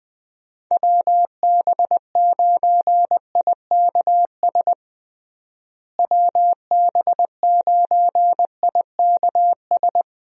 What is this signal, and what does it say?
Morse code, heard as a keyed tone, about 20 wpm, 700 Hz. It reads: WB9IKS WB9IKS
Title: USA Call Signs - 6 Characters - Repeated 2x - 20wpm